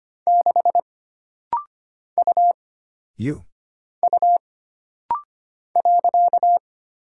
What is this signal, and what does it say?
Morse code, heard as a keyed tone, about 25 wpm, 700 Hz. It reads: 6 U U .
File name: Single Character (Letter:Number:Punctuation:Prosign) - Rapid-Fire - 25wpm